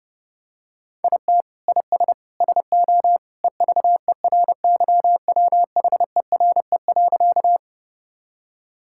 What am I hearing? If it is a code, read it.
Morse code, heard as a keyed tone, about 30 wpm, 700 Hz. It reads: IT IS SO EVERYWHERE.
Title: Sentences from Top 1600 Words 30wpm